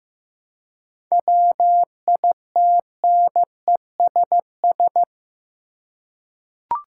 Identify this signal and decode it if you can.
Morse code, heard as a keyed tone, about 15 wpm, 700 Hz. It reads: WITNESS
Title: Seven Letter Words - 15wpm